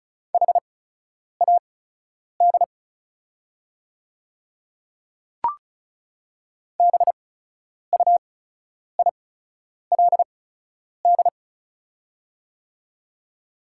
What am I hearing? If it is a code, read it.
Morse code, heard as a keyed tone, about 34 wpm, spaced as for 10 wpm, 700 Hz. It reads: HAD BUILD
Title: 8x Character Spacing - Top 300 Words - 35wpm